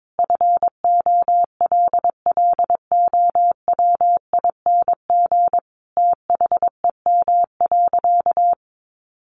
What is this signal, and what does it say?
Morse code, heard as a keyed tone, about 22 wpm, 700 Hz. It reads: FOLLOWING THEM.
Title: Sentences from Top 700 Words 22wpm